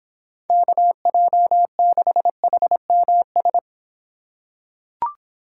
Morse code, heard as a keyed tone, about 26 wpm, 700 Hz. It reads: KJ6HMS